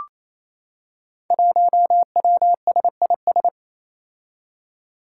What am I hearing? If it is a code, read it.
Morse code, heard as a keyed tone, about 27 wpm, 700 Hz. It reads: 1WSIS
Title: Five Challenging Letter-Number Combinations 28wpm